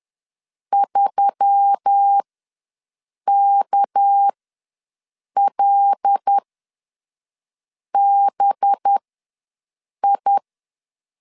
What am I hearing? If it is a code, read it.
Morse code, heard as a keyed tone, about 10 wpm, spaced as for 5 wpm, 800 Hz. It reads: 3KLBI
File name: LEZIONI / 6 PKX0 / 6 PKX0 ws 10wpm 4220